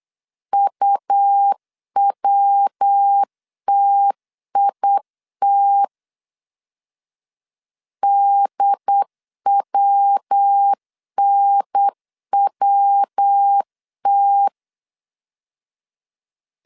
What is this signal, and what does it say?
Morse code, heard as a keyed tone, about 8 wpm, 800 Hz. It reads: UWTIT DWNWT